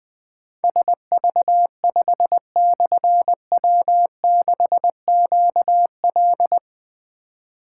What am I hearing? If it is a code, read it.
Morse code, heard as a keyed tone, about 21 wpm, 700 Hz. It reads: SV5/W6QL
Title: Difficult Call Signs 20wpm